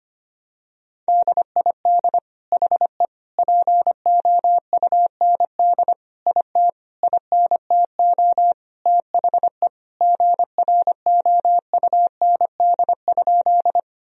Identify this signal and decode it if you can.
Morse code, heard as a keyed tone, about 25 wpm, 700 Hz. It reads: DID HE POUND IT INTO THE GROUND?
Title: Sentences from Top 1000 Words - Review Part 2 - 25wpm